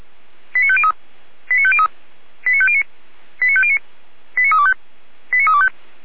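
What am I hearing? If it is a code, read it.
Five-tone selective call (ZVEI), heard as quick runs of five tones, tones of about 70 ms. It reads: 89503 89503 89509 89509 89325 89325
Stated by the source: ZVEI five-tone call received by SDR, server of Feuerwehr Orlamünde fire brigade.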